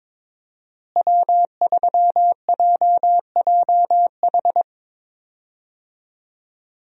Morse code, heard as a keyed tone, about 22 wpm, 700 Hz. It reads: W3JJH
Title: USA Call Signs - 5 Characters - Repeated 2x - 22wpm